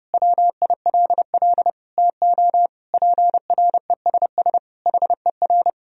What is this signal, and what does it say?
Morse code, heard as a keyed tone, about 32 wpm, 700 Hz. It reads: WILL TO PRESS HER
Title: Sentences from Top 1000 Words - Review Part 1 - 30wpm